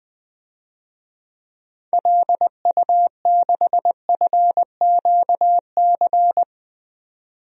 Morse code, heard as a keyed tone, about 21 wpm, 700 Hz. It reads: LU6FQC
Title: DXCC-100 Call Signs - 6 Characters - 20wpm